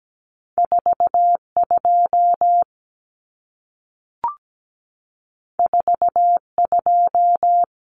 Morse code, heard as a keyed tone, about 17 wpm, 700 Hz. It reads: 42 42
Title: Numbers - 2-Digit 17wpm